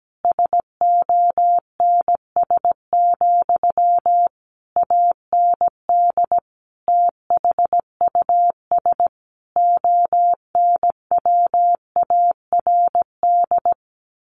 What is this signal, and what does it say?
Morse code, heard as a keyed tone, about 17 wpm, 700 Hz. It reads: SONS, AND THUS ONWARD
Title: A Sound Of Thunder 17wpm